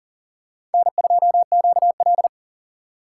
Morse code, heard as a keyed tone, about 40 wpm, 700 Hz. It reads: N2QL
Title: USA Call Signs - 4 Characters - 6x Character Spacing - 40wpm